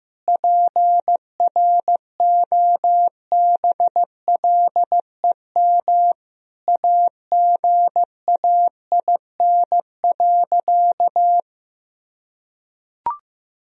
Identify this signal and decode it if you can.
Morse code, heard as a keyed tone, about 15 wpm, 700 Hz. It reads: PROBLEM AGAIN.
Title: Sentences from Top 1000 Words - Review Part 3 - 15wpm